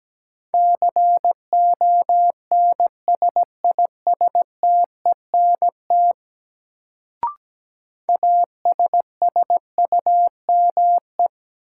Morse code, heard as a keyed tone, about 17 wpm, 700 Hz. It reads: CONSISTENT ASSUME
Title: Sentences from Top 1500 Words - New Word Drill Alt - 17wpm